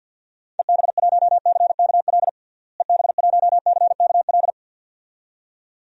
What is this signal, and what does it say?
Morse code, heard as a keyed tone, about 52 wpm, 700 Hz. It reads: EB1CKL EB1CKL
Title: DXCC-100 Call Signs - 6 Characters - Repeated 2x - 50wpm